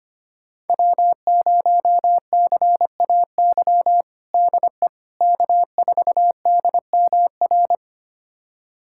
Morse code, heard as a keyed tone, about 25 wpm, 700 Hz. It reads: W0CAY DE K4DMR
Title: USA Callsign DE Callsign - 5 Characters - 25wpm